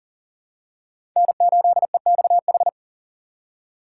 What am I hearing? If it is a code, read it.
Morse code, heard as a keyed tone, about 40 wpm, 700 Hz. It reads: N8EXH
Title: USA Call Signs - 3x Character Spacing - 40wpm